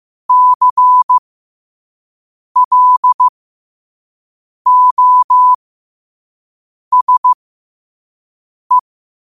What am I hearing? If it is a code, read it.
Morse code, heard as a keyed tone, about 15 wpm, spaced as for 5 wpm, 1000 Hz. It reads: CLOSE
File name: qotdfiles / 5wpm